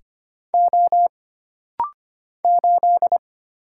Morse code, heard as a keyed tone, about 25 wpm, 700 Hz. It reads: O 8
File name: Single Character (Letter:Number:Punctuation:Prosign) - Rapid-Fire - 25wpm